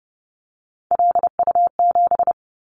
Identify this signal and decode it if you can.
Morse code, heard as a keyed tone, about 30 wpm, 700 Hz. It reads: LU7
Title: Callsign Prefixes for Top 100 DXCC - 30wpm